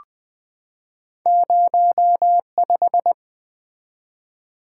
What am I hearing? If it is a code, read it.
Morse code, heard as a keyed tone, about 20 wpm, 700 Hz. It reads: 05
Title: Two Letter-Number Combinations - 20wpm